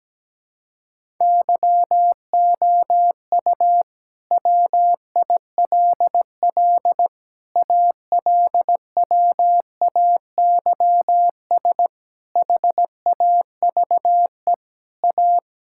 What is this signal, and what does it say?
Morse code, heard as a keyed tone, about 17 wpm, 700 Hz. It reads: YOU WILL ALWAYS HAVE A
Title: Sentences from Top 1600 Words 17wpm